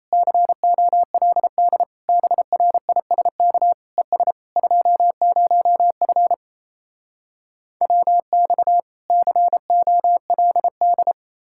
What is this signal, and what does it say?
Morse code, heard as a keyed tone, about 37 wpm, 700 Hz. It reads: COLD BRISK ES 20F WX COLD